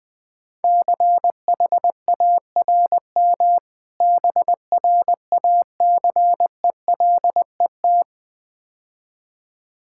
Morse code, heard as a keyed tone, about 20 wpm, 700 Hz. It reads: CHARM BRACELET
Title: Difficult Words - Speed-Racing with Hints - 20wpm